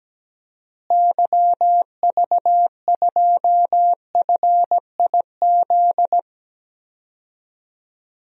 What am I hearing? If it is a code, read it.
Morse code, heard as a keyed tone, about 17 wpm, 700 Hz. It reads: YV2FIZ
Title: DXCC-100 Call Signs - 6 Characters - Repeated 2x - 17wpm